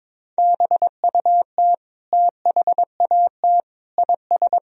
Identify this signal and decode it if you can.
Morse code, heard as a keyed tone, about 21 wpm, 700 Hz. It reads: BUT THAT IS
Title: Sentences from Top 1000 Words - Review Part 4 - 22wpm